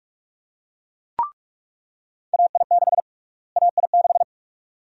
Morse code, heard as a keyed tone, about 44 wpm, 700 Hz. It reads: AI6 AI6
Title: USA Call Sign Prefixes - Repeated 2x - 45wpm